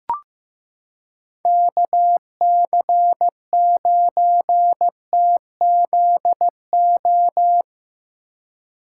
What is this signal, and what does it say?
Morse code, heard as a keyed tone, about 15 wpm, 700 Hz. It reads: KC9TZO
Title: K1USN SST QSO Practice - 15wpm